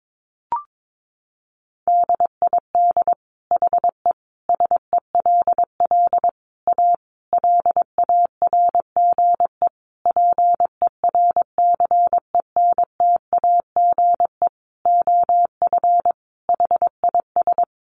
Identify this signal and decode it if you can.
Morse code, heard as a keyed tone, about 22 wpm, 700 Hz. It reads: DID HE SELL A LARGE PERCENTAGE OF HIS
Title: Sentences from Top 1300 Words 22wpm